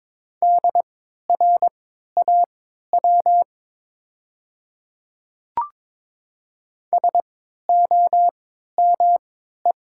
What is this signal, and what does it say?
Morse code, heard as a keyed tone, about 22 wpm, spaced as for 12 wpm, 700 Hz. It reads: DRAW SOME
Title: 3x Character Spacing - Top 300 Words - 22wpm